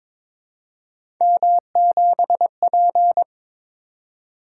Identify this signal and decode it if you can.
Morse code, heard as a keyed tone, about 22 wpm, 700 Hz. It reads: M7P
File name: practice / 22wpm / Three Letter-Number Combinations - 22wpm